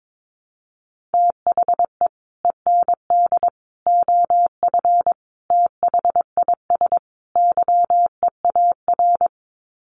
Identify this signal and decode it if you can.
Morse code, heard as a keyed tone, about 22 wpm, 700 Hz. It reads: THE END OF THIS YEAR
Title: Sets of 5 Words Encore 22wpm